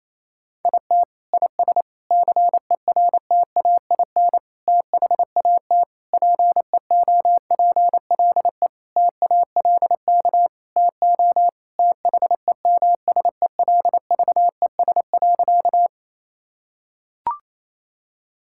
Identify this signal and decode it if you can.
Morse code, heard as a keyed tone, about 28 wpm, 700 Hz. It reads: IT IS CERTAIN THAT PEOPLE TALK TO THEMSELVES.
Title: Sentences from Top 1000 Words - Review Part 5 - 28wpm